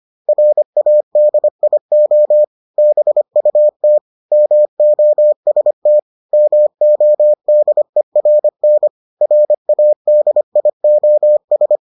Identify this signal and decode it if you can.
Morse code, heard as a keyed tone, about 25 wpm, 600 Hz. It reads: RADIO BUT MOST MODERN RADIOS